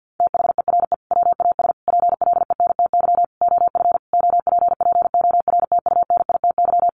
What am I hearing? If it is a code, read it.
Morse code, heard as a keyed tone, about 50 wpm, 700 Hz. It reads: THERE WAS PLENTY OF OPPORTUNITY
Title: Sentences from Top 1300 Words 50wpm